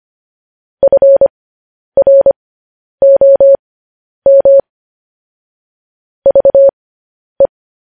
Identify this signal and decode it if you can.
Morse code, heard as a keyed tone, about 25 wpm, spaced as for 10 wpm, 550 Hz. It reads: FROM VE
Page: news-headlines-in-morse-code-at-10-wpm